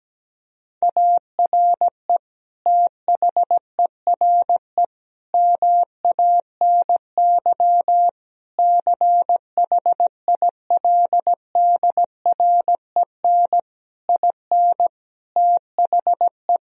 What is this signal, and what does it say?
Morse code, heard as a keyed tone, about 17 wpm, 700 Hz. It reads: ARE THERE MANY CHILDREN IN THE
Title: Sentences from Top 1000 Words - Review Part 4 - 17wpm